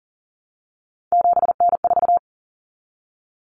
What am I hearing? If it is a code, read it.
Morse code, heard as a keyed tone, about 40 wpm, 700 Hz. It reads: 7N4
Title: Common Cut Numbers - 3-Digit 40wpm